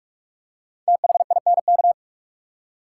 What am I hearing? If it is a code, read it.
Morse code, heard as a keyed tone, about 42 wpm, 700 Hz. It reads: THINK
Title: Five Letter Words - 45wpm